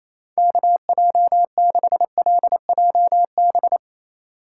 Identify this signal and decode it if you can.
Morse code, heard as a keyed tone, about 27 wpm, 700 Hz. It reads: KJ6LJB
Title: USA Call Signs - 2x Character Spacing - 28wpm